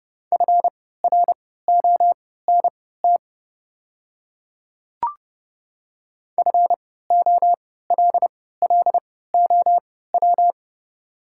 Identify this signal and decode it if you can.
Morse code, heard as a keyed tone, about 30 wpm, spaced as for 17 wpm, 700 Hz. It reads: FRONT FOLLOW
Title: 3x Character Spacing - Top 500 Words - 30wpm